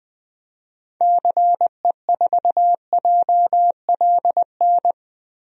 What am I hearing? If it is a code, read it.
Morse code, heard as a keyed tone, about 20 wpm, 700 Hz. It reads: CE4JLN
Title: DXCC-100 Call Signs - 6 Characters - Repeated 3x - 20wpm